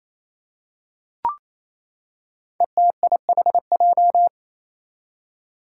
Morse code, heard as a keyed tone, about 28 wpm, 700 Hz. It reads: ETIHJ
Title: Five Challenging Letter-Number Combinations 28wpm